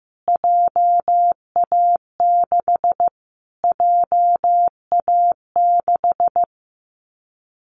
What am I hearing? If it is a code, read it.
Morse code, heard as a keyed tone, about 15 wpm, 700 Hz. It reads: JA6 JA6
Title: Callsign Prefixes for Top 25 DXCC - Repeated 3x - 15wpm